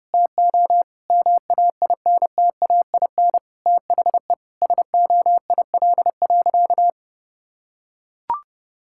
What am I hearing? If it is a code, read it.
Morse code, heard as a keyed tone, about 32 wpm, 700 Hz. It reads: TO MAINTAIN THE SOIL.
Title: Sentences from Top 1000 Words - Review Part 5 - 30wpm